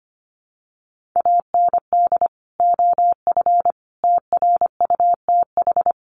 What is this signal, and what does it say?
Morse code, heard as a keyed tone, about 25 wpm, 700 Hz. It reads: AND OF TRUTH
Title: Sentences from Top 1500 Words 25wpm